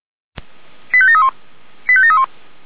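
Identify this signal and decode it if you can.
Five-tone selective call (ZVEI), heard as quick runs of five tones, tones of about 70 ms. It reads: 85731 85731
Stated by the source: ZVEI five-tone selective call received by SDR, from feuerwehr-orlamuende (Orlamünde fire brigade) audio archive.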